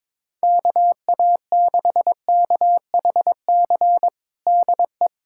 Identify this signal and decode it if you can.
Morse code, heard as a keyed tone, about 22 wpm, 700 Hz. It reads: KA6KHC DE
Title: USA Callsign DE Callsign - 6 Characters - 22wpm